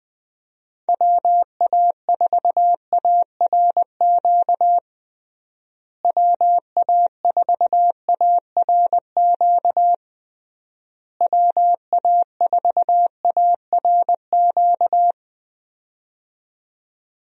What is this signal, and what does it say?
Morse code, heard as a keyed tone, about 20 wpm, 700 Hz. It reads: WA4ARQ WA4ARQ WA4ARQ
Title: USA Call Signs - 6 Characters - Repeated 3x - 20wpm